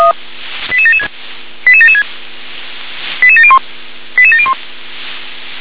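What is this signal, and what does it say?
Five-tone selective call (ZVEI), heard as quick runs of five tones, tones of about 70 ms. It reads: 88776 88776 88701 88701
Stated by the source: ZVEI five-tone call 88761 received by SDR.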